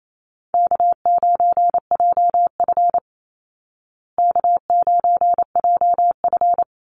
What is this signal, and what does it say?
Morse code, heard as a keyed tone, about 28 wpm, 700 Hz. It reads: K9JF K9JF
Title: Easy Call Signs Repeated 2x - 28wpm